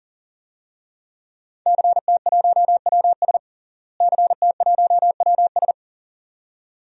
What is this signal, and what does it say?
Morse code, heard as a keyed tone, about 40 wpm, 700 Hz. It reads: CT1WS CT1WS
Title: DXCC-100 Call Signs - 5 Characters - Repeated 2x - 40wpm